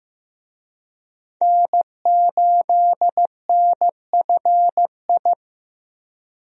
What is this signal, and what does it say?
Morse code, heard as a keyed tone, about 15 wpm, 700 Hz. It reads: N8NFI
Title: USA Call Signs - 5 Characters - 15wpm